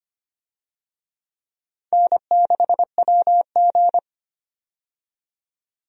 Morse code, heard as a keyed tone, about 25 wpm, 700 Hz. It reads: N6WG
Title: USA Call Signs - 4 Characters - 25wpm